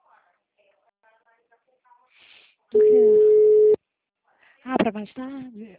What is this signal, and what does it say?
Ringback tone (Europe, 425 Hz), with speech.